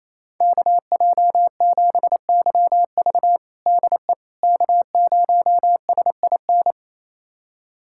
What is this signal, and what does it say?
Morse code, heard as a keyed tone, about 28 wpm, 700 Hz. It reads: KJ7YV DE K0SIN
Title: USA Callsign DE Callsign - 5 Characters - 28wpm